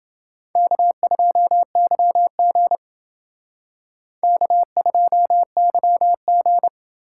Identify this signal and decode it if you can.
Morse code, heard as a keyed tone, about 30 wpm, 700 Hz. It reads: K2YG K2YG